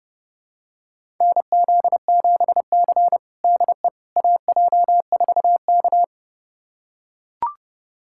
Morse code, heard as a keyed tone, about 30 wpm, 700 Hz. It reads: NZ7C DE AJ4K
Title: USA Callsign DE Callsign - 4 Characters - 30wpm